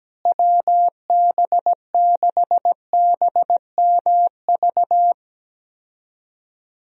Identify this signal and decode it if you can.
Morse code, heard as a keyed tone, about 17 wpm, 700 Hz. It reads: WB6BMV
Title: USA Call Signs - 6 Characters - Repeated 2x - 17wpm